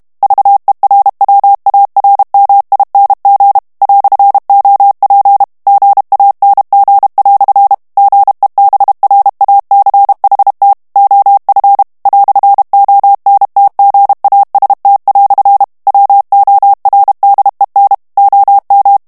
Morse code, heard as a keyed tone, about 32 wpm, 800 Hz. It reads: VERWARMING "OP GANG" GEBRACHT OF "ONTGAST" WORDEN OM
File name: morse mp3 / HOLLANDAIS texte clair 2 vitesse 30 mots minute